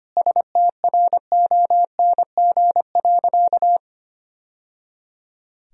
Morse code, heard as a keyed tone, about 25 wpm, 700 Hz. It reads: STRONG.
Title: Sentences from Top 1000 Words - Review Part 2 - 25wpm